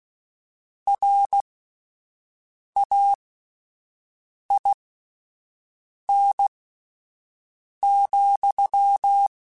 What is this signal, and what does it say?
Morse code, heard as a keyed tone, about 16 wpm, spaced as for 5 wpm, 800 Hz. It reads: RAIN,
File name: Spaced Characters / Spaced Characters-mp3s / Weather5 5 sp @10